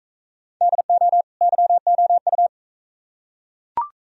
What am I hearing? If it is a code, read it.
Morse code, heard as a keyed tone, about 41 wpm, 700 Hz. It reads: DO YOU